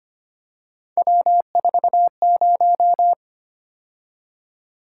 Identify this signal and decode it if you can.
Morse code, heard as a keyed tone, about 25 wpm, 700 Hz. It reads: W40
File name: Three Letter-Number Combinations - 25wpm